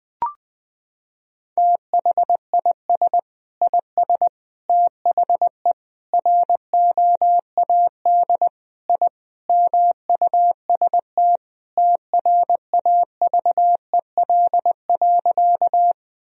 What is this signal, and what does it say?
Morse code, heard as a keyed tone, about 20 wpm, 700 Hz. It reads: THIS IS THE ROAD I MUST TRAVEL.
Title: Sentences from Top 1000 Words - Review Part 2 - 20wpm